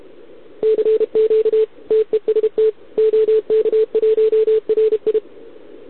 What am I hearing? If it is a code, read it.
Morse code, heard as a keyed tone, about 31 wpm, 440 Hz. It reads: CQ TEST OK1RI
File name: ARCHIV / OKDX2020 / OK1RIb